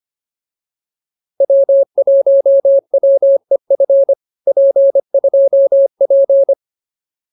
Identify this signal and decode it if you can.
Morse code, heard as a keyed tone, about 24 wpm, 550 Hz. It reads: W1WEF P2P
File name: POTA306a 25